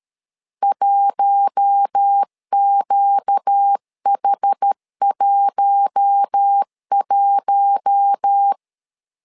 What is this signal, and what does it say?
Morse code, heard as a keyed tone, about 13 wpm, 800 Hz. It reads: 1QH11
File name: LEZIONI / 4 Q1 12wpm 2028